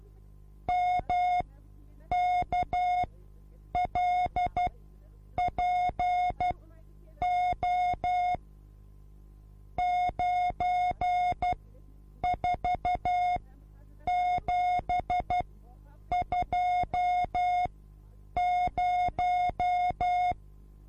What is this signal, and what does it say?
Morse code, heard as a keyed tone, about 12 wpm, spaced as for 8 wpm, 720 Hz. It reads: MKLPO 94720
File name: repos / Morsekurs / DJ4UF Morsen24